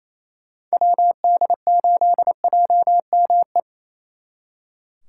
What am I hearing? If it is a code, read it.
Morse code, heard as a keyed tone, about 36 wpm, 700 Hz. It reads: WD8JME